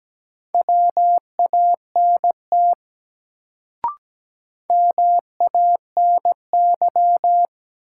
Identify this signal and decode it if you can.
Morse code, heard as a keyed tone, about 17 wpm, 700 Hz. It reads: WANT MANY
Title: Sets of 2 Words - New Word Drill - 17wpm